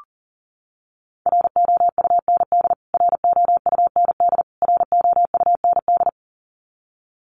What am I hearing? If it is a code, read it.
Morse code, heard as a keyed tone, about 40 wpm, 700 Hz. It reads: ROUND ROUND ROUND
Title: Top 200 Words Repeated 40wpm